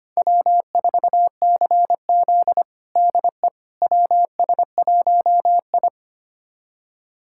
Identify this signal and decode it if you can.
Morse code, heard as a keyed tone, about 25 wpm, 700 Hz. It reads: W4CZ DE WS1I
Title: USA Callsign DE Callsign - 4 Characters - 25wpm